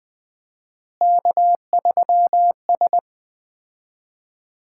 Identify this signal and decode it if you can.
Morse code, heard as a keyed tone, about 21 wpm, 700 Hz. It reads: K3S
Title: Common QSO Phrases - 4x Word-Spacing - 20wpm